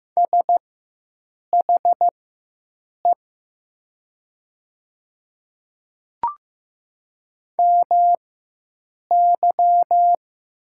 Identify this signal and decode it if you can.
Morse code, heard as a keyed tone, about 15 wpm, spaced as for 7 wpm, 700 Hz. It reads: SHE MY
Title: 4x Character Spacing - Top 100 Words - 15wpm